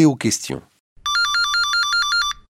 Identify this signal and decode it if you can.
Speech. There is no code.